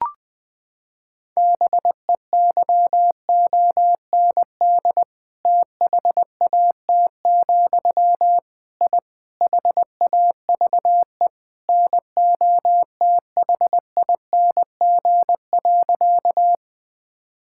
Morse code, heard as a keyed tone, about 20 wpm, 700 Hz. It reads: BEYOND THAT, I HAVE NOTHING.
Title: Sentences from Top 1000 Words - Review Part 6 - 20wpm